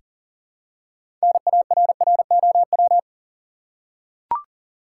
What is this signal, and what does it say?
Morse code, heard as a keyed tone, about 40 wpm, 700 Hz. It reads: NARROW
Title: Six Letter Words - 40wpm